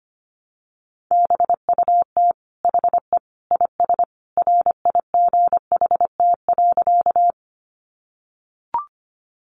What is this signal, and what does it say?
Morse code, heard as a keyed tone, about 25 wpm, 700 Hz. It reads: BUT HE IS RIGHT.
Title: Sentences from Top 200 Words 25wpm